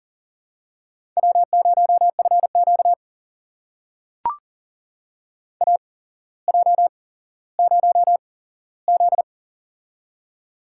Morse code, heard as a keyed tone, about 40 wpm, spaced as for 17 wpm, 700 Hz. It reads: W0FQ AJ0Z
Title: USA Call Signs - 4 Characters - 8x Character Spacing - 40wpm